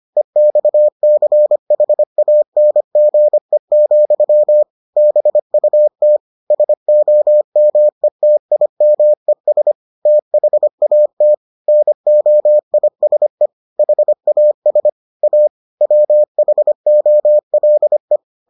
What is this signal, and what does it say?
Morse code, heard as a keyed tone, about 25 wpm, 600 Hz. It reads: EXCHANGE, BUT SOMETIMES THAT NOISE HAS A WHOLE